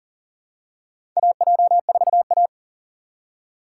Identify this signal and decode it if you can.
Morse code, heard as a keyed tone, about 39 wpm, 700 Hz. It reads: AJ4A
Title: USA Call Signs - 4 Characters - 4x Character Spacing - 40wpm